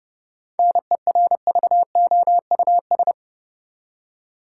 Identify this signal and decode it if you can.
Morse code, heard as a keyed tone, about 30 wpm, 700 Hz. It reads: NERVOUS
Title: Seven Letter Words - 30wpm